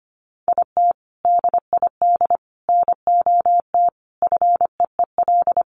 Morse code, heard as a keyed tone, about 25 wpm, 700 Hz. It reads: IT DID NOT FEEL
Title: Sentences from Top 1100 Words 25wpm